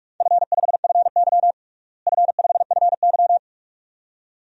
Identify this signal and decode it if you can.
Morse code, heard as a keyed tone, about 46 wpm, 700 Hz. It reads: F5FY F5FY
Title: DXCC-100 Call Signs - 4 Characters - Repeated 2x - 45wpm